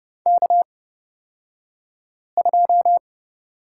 Morse code, heard as a keyed tone, about 30 wpm, spaced as for 5 wpm, 700 Hz. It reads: K2
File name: Lesson 38d - Letter q - Two Characters 30wpm